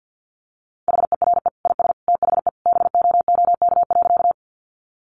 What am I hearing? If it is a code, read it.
Morse code, heard as a keyed tone, about 50 wpm, 700 Hz. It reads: HERE IS THE BOOK.
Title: Sentences from Top 300 Words 50wpm